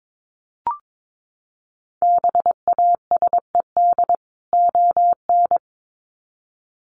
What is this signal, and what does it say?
Morse code, heard as a keyed tone, about 22 wpm, 700 Hz. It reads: BASED ON
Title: Sets of 2 Words 22wpm